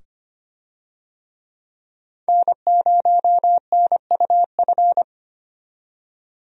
Morse code, heard as a keyed tone, about 25 wpm, 700 Hz. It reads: N0NUF
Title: USA Call Signs - 5 Characters - 25wpm